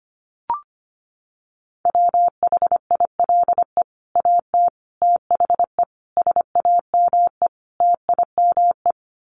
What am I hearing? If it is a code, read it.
Morse code, heard as a keyed tone, about 25 wpm, 700 Hz. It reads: WHILE AT THE SAME TIME
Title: Sets of 5 Words 25wpm